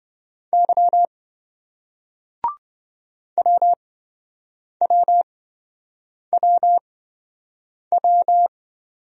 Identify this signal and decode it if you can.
Morse code, heard as a keyed tone, about 27 wpm, spaced as for 7 wpm, 700 Hz. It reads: Y WWWW